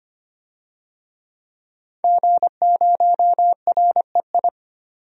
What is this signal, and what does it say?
Morse code, heard as a keyed tone, about 25 wpm, 700 Hz. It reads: G0REI
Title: DXCC-100 Call Signs - 5 Characters - 25wpm